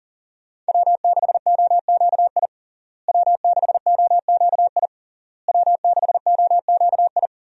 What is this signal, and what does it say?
Morse code, heard as a keyed tone, about 40 wpm, 700 Hz. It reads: W6OQI W6OQI W6OQI
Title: USA Call Signs - 5 Characters - Repeated 3x - 40wpm